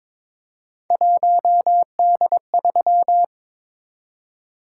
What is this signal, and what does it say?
Morse code, heard as a keyed tone, about 22 wpm, 700 Hz. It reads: 1D3
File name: practice / 22wpm / Three Letter-Number Combinations - 22wpm